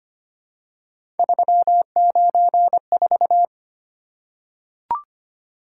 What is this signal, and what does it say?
Morse code, heard as a keyed tone, about 25 wpm, 700 Hz. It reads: 394
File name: Three Letter-Number Combinations - 25wpm